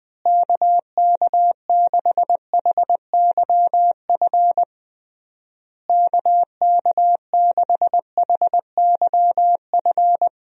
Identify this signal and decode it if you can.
Morse code, heard as a keyed tone, about 20 wpm, 700 Hz. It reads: KK6HYF KK6HYF
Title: USA Call Signs - 6 Characters - Repeated 3x - 20wpm